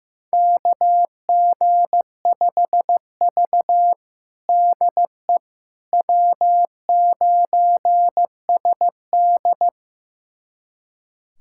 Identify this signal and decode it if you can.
Morse code, heard as a keyed tone, about 15 wpm, 700 Hz. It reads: KG5V DE W9SD
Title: USA Callsign DE Callsign - 4 Characters - 15wpm